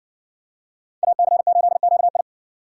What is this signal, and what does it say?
Morse code, heard as a keyed tone, about 60 wpm, 700 Hz. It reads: AC8QI